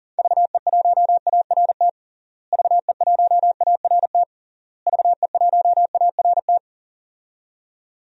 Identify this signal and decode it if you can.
Morse code, heard as a keyed tone, about 40 wpm, 700 Hz. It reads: VE1ART VE1ART VE1ART
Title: DXCC-100 Call Signs - 6 Characters - Repeated 3x - 40wpm